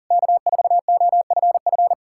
Morse code, heard as a keyed tone, about 53 wpm, 700 Hz. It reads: K4OFF